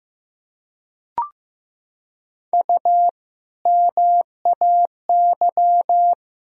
Morse code, heard as a keyed tone, about 15 wpm, 700 Hz. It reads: U MAY
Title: Sentences from Top 100 Words and Abbreviations 15wpm